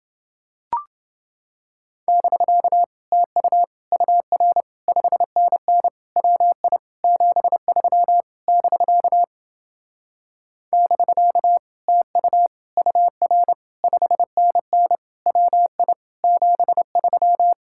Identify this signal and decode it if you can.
Morse code, heard as a keyed tone, about 31 wpm, 700 Hz. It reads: <BK> TU UR 5NN WI 73 <BK> <BK> TU UR 5NN WI 73